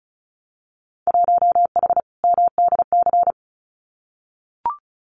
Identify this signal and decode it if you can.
Morse code, heard as a keyed tone, about 36 wpm, 690 Hz. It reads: 1H MDC
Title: Winter Field Day Exchanges 35wpm